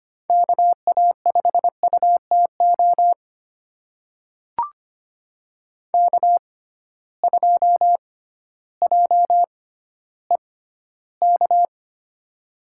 Morse code, heard as a keyed tone, about 25 wpm, spaced as for 13 wpm, 700 Hz. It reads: KA5UTO K2JEK
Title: USA Call Signs - 6x Character Spacing - 25wpm